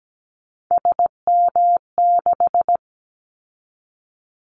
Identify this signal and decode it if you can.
Morse code, heard as a keyed tone, about 17 wpm, 700 Hz. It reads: SM6
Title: Callsign Prefixes for Top 25 DXCC - 17wpm